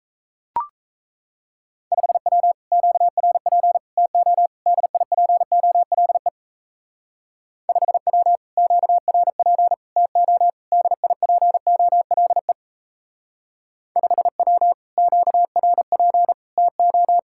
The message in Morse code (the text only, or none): 5W QRP TO DIPOLE 5W QRP TO DIPOLE 5W QRP TO